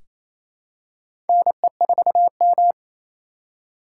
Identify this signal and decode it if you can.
Morse code, heard as a keyed tone, about 28 wpm, 700 Hz. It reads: NE4M
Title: K1USN SST QSO Practice - 28wpm